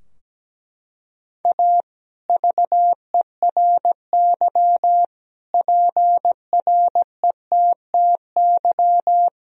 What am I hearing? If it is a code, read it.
Morse code, heard as a keyed tone, about 17 wpm, 700 Hz. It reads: A VERY PRETTY
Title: Sentences from Top 1000 Words - Review Part 2 - 17wpm